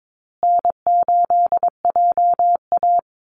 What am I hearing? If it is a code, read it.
Morse code, heard as a keyed tone, about 22 wpm, 700 Hz. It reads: N8JA
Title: Call Signs Repeated 2x - 22wpm